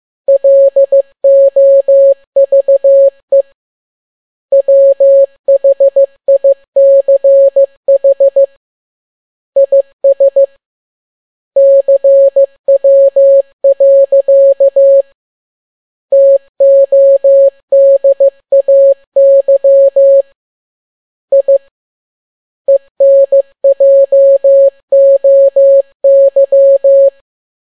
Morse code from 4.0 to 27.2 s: WHICH IS CW. TODAY I ENJOY